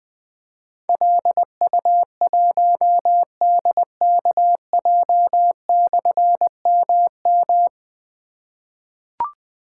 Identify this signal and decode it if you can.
Morse code, heard as a keyed tone, about 20 wpm, 700 Hz. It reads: LU1DKJ/MM
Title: Difficult Call Signs 20wpm